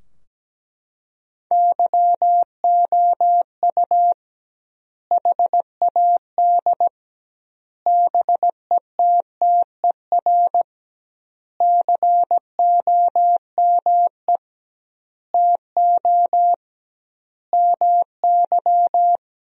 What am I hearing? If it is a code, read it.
Morse code, heard as a keyed tone, about 17 wpm, 700 Hz. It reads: YOU HAD BETTER COME TO MY
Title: 2x Word-Spacing - Sentences from Top 400 Words - 17wpm